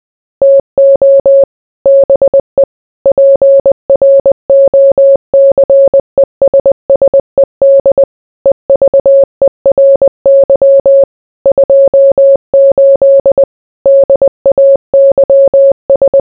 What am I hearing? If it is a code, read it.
Morse code, heard as a keyed tone, about 20 wpm, 550 Hz. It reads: TO BE PROCESSED EVERY 28 DAYS